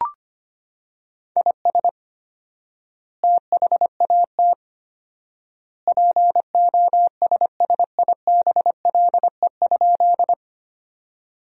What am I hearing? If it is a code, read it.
Morse code, heard as a keyed tone, about 25 wpm, 700 Hz. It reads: IS THAT POSSIBLE?
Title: 4x Word-Spacing - Sentences from Top 500 Words - 25wpm